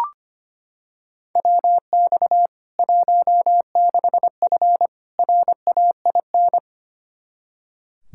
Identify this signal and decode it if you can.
Morse code, heard as a keyed tone, about 25 wpm, 700 Hz. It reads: WX 16F RAIN
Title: K1USN SST QSO Practice 25wpm